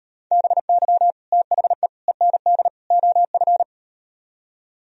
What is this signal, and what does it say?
Morse code, heard as a keyed tone, about 37 wpm, 700 Hz. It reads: BY THE END OF